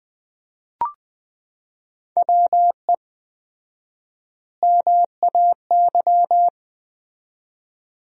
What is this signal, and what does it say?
Morse code, heard as a keyed tone, about 20 wpm, 700 Hz. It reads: WE MAY
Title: 4x Word-Spacing - Sentences from Top 300 Words - 20wpm